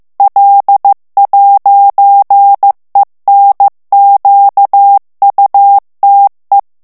Morse code, heard as a keyed tone, about 15 wpm, 800 Hz. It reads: L'ENQUTE
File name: morse mp3 / FRANCAIS TEXTE CLAIR 1 VITESSE 14 MOTS MINUTES